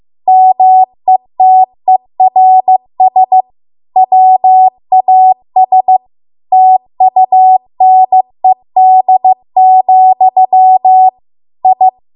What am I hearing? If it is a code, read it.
Morse code, heard as a keyed tone, about 15 wpm, 750 Hz. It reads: METERS WAS TUNED, I